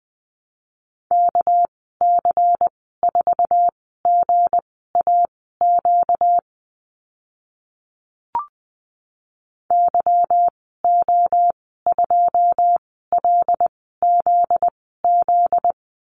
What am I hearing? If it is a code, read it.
Morse code, heard as a keyed tone, about 20 wpm, spaced as for 14 wpm, 700 Hz. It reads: KC4GAQ YO2LZZ
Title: 245 Lesson 39g - Letter z - Call Signs 20wpm